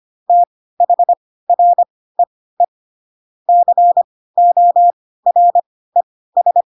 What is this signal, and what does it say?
Morse code, heard as a keyed tone, about 25 wpm, spaced as for 16 wpm, 700 Hz. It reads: THREE CORES
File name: Pinocchio / Pinocchio ch7 10 25 ik1hge